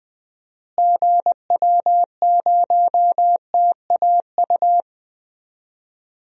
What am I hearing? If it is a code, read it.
Morse code, heard as a keyed tone, about 20 wpm, 700 Hz. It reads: GW0TAU
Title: DXCC-100 Call Signs - 6 Characters - Repeated 3x - 20wpm